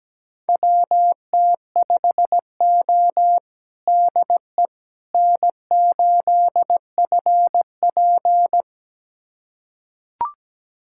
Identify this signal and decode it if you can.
Morse code, heard as a keyed tone, about 17 wpm, 700 Hz. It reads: WT5O DE N8FP
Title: USA Callsign DE Callsign - 4 Characters - 17wpm